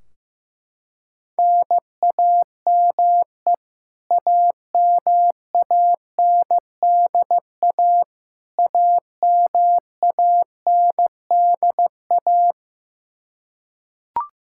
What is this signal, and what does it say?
Morse code, heard as a keyed tone, about 15 wpm, 700 Hz. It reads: NAME AMANDA AMANDA
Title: K1USN SST QSO Practice 15wpm